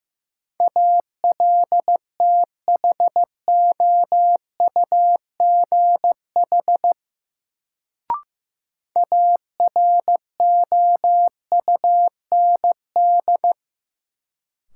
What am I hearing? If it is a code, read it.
Morse code, heard as a keyed tone, about 15 wpm, 700 Hz. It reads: ALTHOUGH AROUND
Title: Sentences from Top 600 Words - New Word Drill - 15wpm